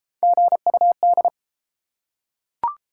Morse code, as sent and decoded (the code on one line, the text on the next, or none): --. ..- -..
GUD